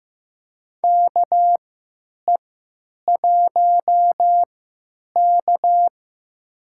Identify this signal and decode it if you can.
Morse code, heard as a keyed tone, about 15 wpm, spaced as for 9 wpm, 700 Hz. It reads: KE1K
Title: USA Call Signs - 3x Character Spacing - 15wpm